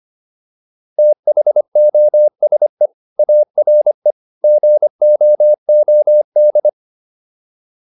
Morse code, heard as a keyed tone, about 25 wpm, 600 Hz. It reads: THOSE ARE GOOD